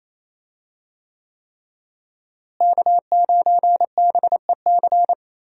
Morse code, heard as a keyed tone, about 28 wpm, 700 Hz. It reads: K9BEC
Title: USA Call Signs - 5 Characters - Repeated 3x - 28wpm